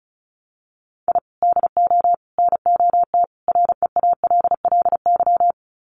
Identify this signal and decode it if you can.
Morse code, heard as a keyed tone, about 34 wpm, 700 Hz. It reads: I DO NOT REALLY
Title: Sets of 4 Words 35wpm